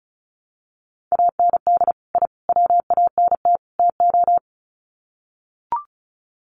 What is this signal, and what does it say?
Morse code, heard as a keyed tone, about 34 wpm, 700 Hz. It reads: AND I WANT TO
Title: Sets of 4 Words 35wpm